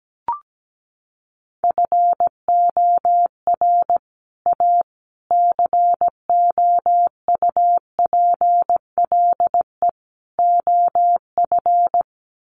Morse code, heard as a keyed tone, about 17 wpm, 700 Hz. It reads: FOR A COUPLE OF
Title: Sets of 4 Words 17wpm